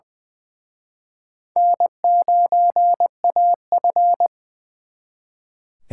Morse code, heard as a keyed tone, about 24 wpm, 700 Hz. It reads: N9AF
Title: Speed Racing - USA 1x2 Call Signs 20wpm